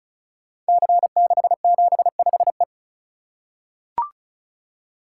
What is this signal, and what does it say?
Morse code, heard as a keyed tone, about 36 wpm, 700 Hz. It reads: C675E
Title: Code Groups - 5 Letter:Numbers - 35wpm